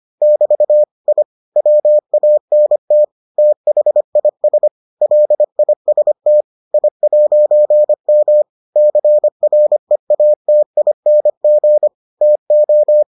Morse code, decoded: <BT> I WANT THIS LIST I'M CREATING TO

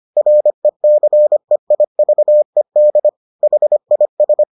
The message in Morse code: RECEIVED HIS

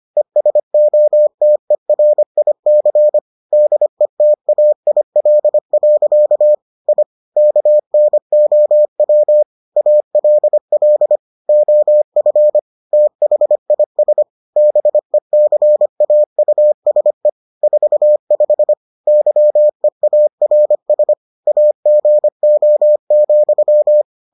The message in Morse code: ESOTERIC DETAIL. I KNOW ALL OF THIS BECAUSE 45 YEARS AGO,

